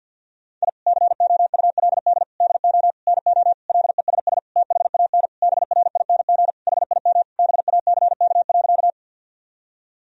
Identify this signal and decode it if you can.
Morse code, heard as a keyed tone, about 50 wpm, 700 Hz. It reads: I COULD DO NO LESS THAN BRING HIM BACK.
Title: Sentences from Top 1000 Words - Review Part 3 - 50wpm